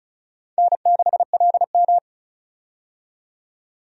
Morse code, heard as a keyed tone, about 36 wpm, 700 Hz. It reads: N6LM
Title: USA Call Signs - 4 Characters - 35wpm